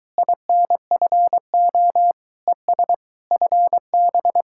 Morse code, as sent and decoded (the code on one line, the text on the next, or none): .. -. ..-. --- / . ... / ..-. -...
INFO ES FB